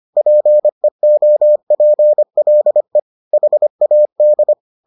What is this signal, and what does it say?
Morse code, heard as a keyed tone, about 25 wpm, 600 Hz. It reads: PEOPLE HAD